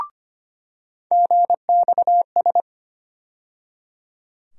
Morse code, heard as a keyed tone, about 25 wpm, 700 Hz. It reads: GXS
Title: USA Call Sign Postfixes - 3 Letters 25wpm